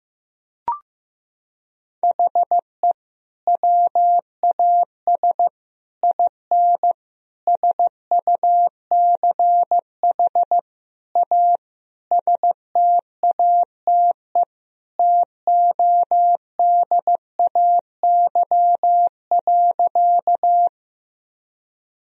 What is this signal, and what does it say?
Morse code, heard as a keyed tone, about 16 wpm, 700 Hz. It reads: HE WAS IN SUCH A STATE TODAY.
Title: Sentences from Top 1000 Words - Review Part 2 - 15wpm